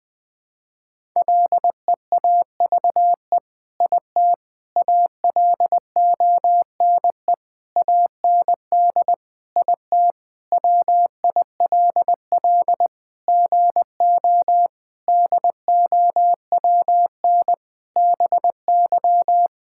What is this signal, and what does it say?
Morse code, heard as a keyed tone, about 20 wpm, 700 Hz. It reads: LEAVE IT ALONE AND IT WILL GO DOWN BY
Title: Sentences from Top 1000 Words - Review Part 5 - 20wpm